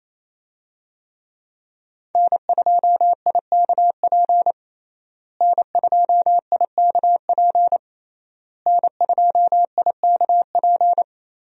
Morse code, heard as a keyed tone, about 27 wpm, 700 Hz. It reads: N2IKP N2IKP N2IKP
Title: USA Call Signs - 5 Characters - Repeated 3x - 28wpm